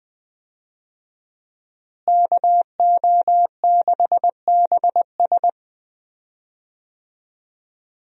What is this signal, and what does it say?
Morse code, heard as a keyed tone, about 20 wpm, 700 Hz. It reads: KO6BS
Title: USA Call Signs - 5 Characters - Repeated 3x - 20wpm